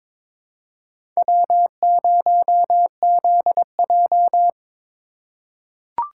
Morse code, heard as a keyed tone, about 22 wpm, 700 Hz. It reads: W0ZJ
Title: USA Call Signs - 4 Characters - 4x Character Spacing - 22wpm